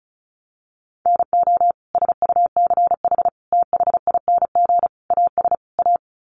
Morse code, heard as a keyed tone, about 34 wpm, 700 Hz. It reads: NO SUCH THING AS A
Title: Sets of 5 Words Encore 35wpm